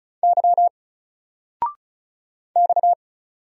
Morse code, heard as a keyed tone, about 36 wpm, 700 Hz. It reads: Y X
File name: Single-Letter-Number-35wpm